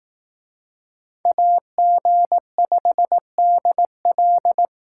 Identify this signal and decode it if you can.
Morse code, heard as a keyed tone, about 18 wpm, 700 Hz. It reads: AG5DL
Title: Speed Racing - USA Call Signs 15wpm